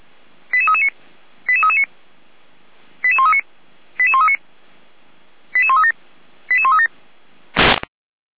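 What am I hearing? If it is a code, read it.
Five-tone selective call (ZVEI), heard as quick runs of five tones, tones of about 70 ms. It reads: 88339 88339 88139 88139 88137 88137
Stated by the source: ZVEI five-tone call received by SDR, from feuerwehr-orlamuende.org (fire brigade Orlamünde).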